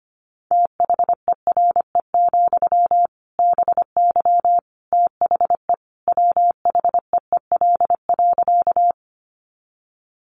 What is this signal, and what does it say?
Morse code, heard as a keyed tone, about 25 wpm, 700 Hz. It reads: THERE, BY THE WHEEL.
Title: Sentences from Top 500 Words 25wpm